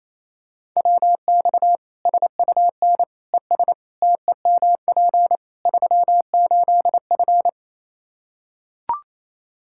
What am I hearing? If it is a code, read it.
Morse code, heard as a keyed tone, about 28 wpm, 700 Hz. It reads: WX SUN ES TEMP 38F
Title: Practice QSO - Exchange 2 - Info - 28wpm